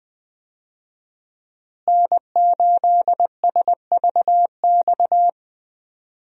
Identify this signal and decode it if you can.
Morse code, heard as a keyed tone, about 20 wpm, 700 Hz. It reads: N8SVX
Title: USA Call Signs - 5 Characters - Repeated 2x - 20wpm